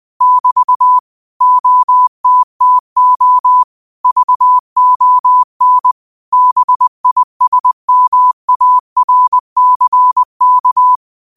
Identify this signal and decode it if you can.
Morse code, heard as a keyed tone, about 20 wpm, 1000 Hz. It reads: <BT> OTTO VON BISMARCK
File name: qotdfiles / 20wpm